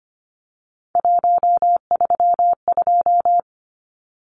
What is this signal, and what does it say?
Morse code, heard as a keyed tone, about 25 wpm, 700 Hz. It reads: 132
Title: Numbers - 3-Digit 25wpm